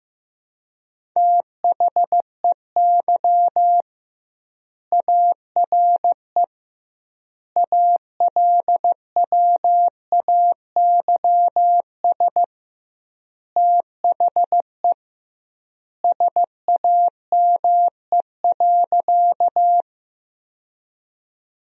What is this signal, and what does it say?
Morse code, heard as a keyed tone, about 15 wpm, 700 Hz. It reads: THEY ARE ALWAYS THE SAME.
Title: 2x Word-Spacing - Sentences from Top 300 Words - 15wpm